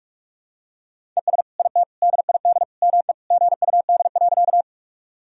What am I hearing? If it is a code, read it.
Morse code, heard as a keyed tone, about 47 wpm, 700 Hz. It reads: ES IT DID ME GUD.
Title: Sentences from Top 200 Words and Abbreviations 45wpm